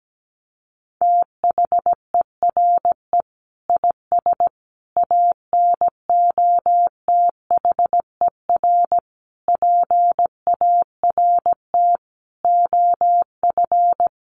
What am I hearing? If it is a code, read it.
Morse code, heard as a keyed tone, about 17 wpm, 700 Hz. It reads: THERE IS ANOTHER PART OF
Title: Sentences from Top 1300 Words 17wpm